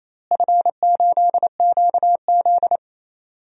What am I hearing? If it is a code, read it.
Morse code, heard as a keyed tone, about 28 wpm, 700 Hz. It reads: F8QZ
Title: DXCC-100 Call Signs - 4 Characters - Repeated 2x - 28wpm